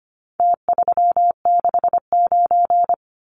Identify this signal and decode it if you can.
Morse code, heard as a keyed tone, about 25 wpm, 700 Hz. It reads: T369
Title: Numbers - Contest Style - 4-Digit 25wpm